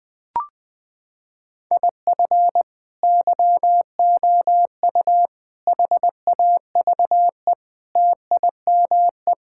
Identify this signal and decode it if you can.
Morse code, heard as a keyed tone, about 21 wpm, 700 Hz. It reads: IF YOU HAVE TIME
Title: Sentences with 3-Word Phrases - 20wpm